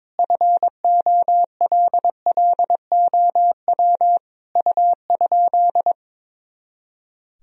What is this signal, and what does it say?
Morse code, heard as a keyed tone, about 21 wpm, 700 Hz. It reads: FOLLOW U?
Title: Sentences from Top 200 Words and Abbreviations 22wpm